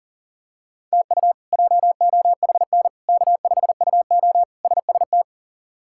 Morse code, heard as a keyed tone, about 37 wpm, 700 Hz. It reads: TU JOHN K5UO SST
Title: K1USN SST QSO Practice - 40wpm